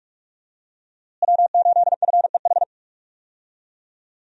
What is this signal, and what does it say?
Morse code, heard as a keyed tone, about 42 wpm, 700 Hz. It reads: W8FEH
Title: USA Call Signs - 5 Characters - Repeated 2x - 45wpm